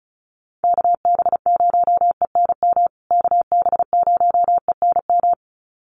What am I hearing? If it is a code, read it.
Morse code, heard as a keyed tone, about 34 wpm, 700 Hz. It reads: KB0ENM KB0ENM
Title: USA Call Signs Repeated 2x 35wpm